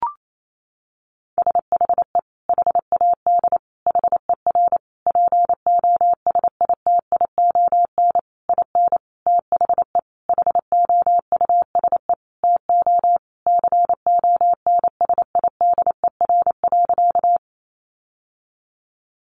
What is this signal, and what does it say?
Morse code, heard as a keyed tone, about 28 wpm, 700 Hz. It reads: SHE HAD HER POSITION IN THE HOUSE TO CONSIDER.
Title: Sentences from Top 900 Words 28wpm